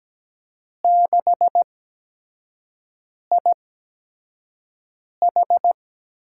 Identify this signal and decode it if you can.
Morse code, heard as a keyed tone, about 17 wpm, spaced as for 5 wpm, 700 Hz. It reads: 6IH
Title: USA Call Signs - 4 Characters - 8x Character Spacing - 17wpm